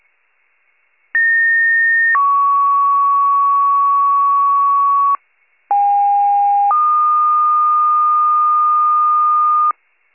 Two-tone page, heard as a one-second tone, then a three-second tone, about 1850 then 1120 Hz, about 810 then 1220 Hz. It is heard from 1.1 to 9.7 s.
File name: wcalert